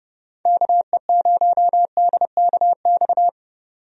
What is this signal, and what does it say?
Morse code, heard as a keyed tone, about 31 wpm, 700 Hz. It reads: KE0DKX
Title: USA Call Signs - 6 Characters - 30wpm